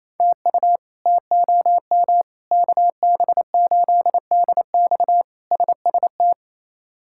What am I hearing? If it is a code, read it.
Morse code, heard as a keyed tone, about 28 wpm, 700 Hz. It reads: TU TOM KB8DX SST
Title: K1USN SST QSO Practice - 28wpm